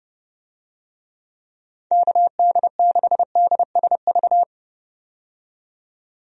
Morse code, heard as a keyed tone, about 31 wpm, 700 Hz. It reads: KD6DSV
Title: USA Call Signs - 6 Characters - 30wpm